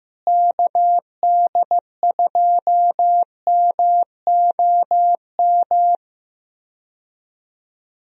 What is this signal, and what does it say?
Morse code, heard as a keyed tone, about 15 wpm, 700 Hz. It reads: KD2MOM
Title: USA Call Signs - 6 Characters - Repeated 3x - 15wpm